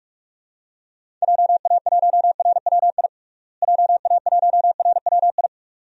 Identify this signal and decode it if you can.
Morse code, heard as a keyed tone, about 42 wpm, 700 Hz. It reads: JA1RWI JA1RWI
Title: USA Call Signs - 6 Characters - Repeated 2x - 45wpm